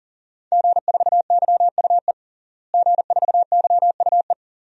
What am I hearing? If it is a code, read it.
Morse code, heard as a keyed tone, about 40 wpm, 700 Hz. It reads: G4YUE G4YUE
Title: DXCC-100 Call Signs - 5 Characters - Repeated 2x - 40wpm